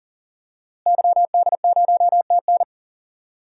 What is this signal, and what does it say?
Morse code, heard as a keyed tone, about 37 wpm, 700 Hz. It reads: YD0TN
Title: DXCC-100 Call Signs - 5 Characters - Repeated 3x - 40wpm